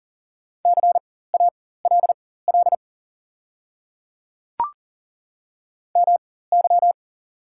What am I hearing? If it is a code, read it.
Morse code, heard as a keyed tone, about 37 wpm, spaced as for 18 wpm, 700 Hz. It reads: CALL MY
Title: 4x Character Spacing - Top 100 Words - 40wpm